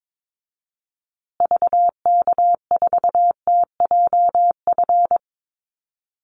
Morse code, heard as a keyed tone, about 22 wpm, 700 Hz. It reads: VK4TJF
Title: Call Signs Repeated 3x - 22wpm